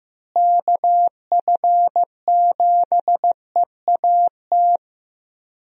Morse code, heard as a keyed tone, about 15 wpm, 700 Hz. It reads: KF7EAT
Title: USA Call Signs - 6 Characters - Repeated 3x - 15wpm